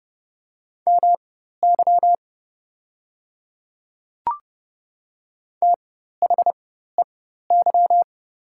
Morse code, heard as a keyed tone, about 30 wpm, spaced as for 14 wpm, 700 Hz. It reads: MY THEY
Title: 4x Character Spacing - Top 100 Words - 30wpm